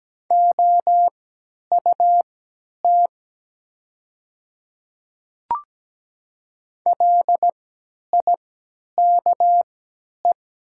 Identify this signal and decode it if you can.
Morse code, heard as a keyed tone, about 17 wpm, spaced as for 10 wpm, 700 Hz. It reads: OUT LIKE
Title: 3x Character Spacing - Top 100 Words - 17wpm